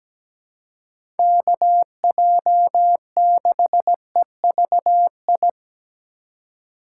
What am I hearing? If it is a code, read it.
Morse code, heard as a keyed tone, about 17 wpm, 700 Hz. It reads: KJ6EVI
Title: USA Call Signs - 6 Characters - 3x Character Spacing - 17wpm